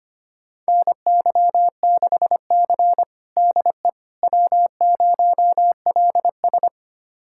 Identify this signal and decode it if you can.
Morse code, heard as a keyed tone, about 25 wpm, 700 Hz. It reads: NY6C DE W0LS
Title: USA Callsign DE Callsign - 4 Characters - 25wpm